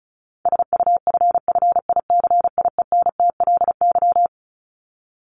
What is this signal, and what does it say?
Morse code, heard as a keyed tone, about 34 wpm, 700 Hz. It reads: SUFFICIENTLY
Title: LY Suffix - Top 100 Words with Base Words 35wpm